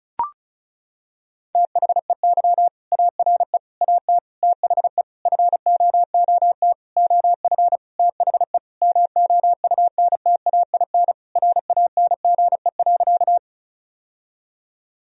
Morse code, heard as a keyed tone, about 34 wpm, 700 Hz. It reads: THEY ARE AT THE FOOT OF THE MOUNTAIN RANGE.
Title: Sentences from Top 1000 Words - Review Part 4 - 35wpm